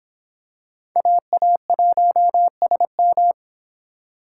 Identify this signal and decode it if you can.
Morse code, heard as a keyed tone, about 25 wpm, 700 Hz. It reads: AA1SM